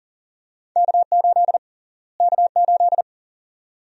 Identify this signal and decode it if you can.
Morse code, heard as a keyed tone, about 40 wpm, 700 Hz. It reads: K8 K8
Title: USA Call Sign Prefixes - Repeated 2x - 40wpm